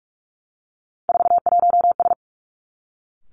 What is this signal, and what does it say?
Morse code, heard as a keyed tone, about 46 wpm, 700 Hz. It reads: 41S
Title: Three Challenging Letter-Number Combinations 45wpm